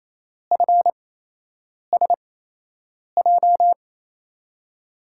Morse code, heard as a keyed tone, about 27 wpm, spaced as for 8 wpm, 700 Hz. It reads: FSJ
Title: USA Call Signs - 6 Characters - 8x Character Spacing - 28wpm